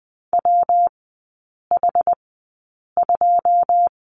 Morse code, heard as a keyed tone, about 20 wpm, spaced as for 8 wpm, 690 Hz. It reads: WH2
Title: 166 Lesson 27e - Number 7 - Three Characters 20wpm